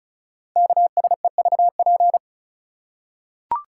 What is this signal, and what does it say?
Morse code, heard as a keyed tone, about 36 wpm, 690 Hz. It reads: KSEVP
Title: Code Groups - 5 Letter:Numbers - 35wpm